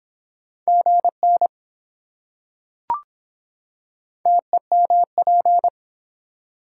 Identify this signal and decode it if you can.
Morse code, heard as a keyed tone, about 23 wpm, 700 Hz. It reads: GN TEMP